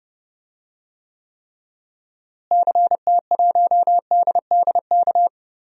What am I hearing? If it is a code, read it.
Morse code, heard as a keyed tone, about 30 wpm, 700 Hz. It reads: CT1DDK
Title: DXCC-100 Call Signs - 6 Characters - Repeated 3x - 30wpm